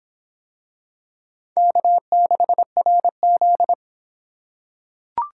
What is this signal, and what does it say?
Morse code, heard as a keyed tone, about 24 wpm, 700 Hz. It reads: K6RZ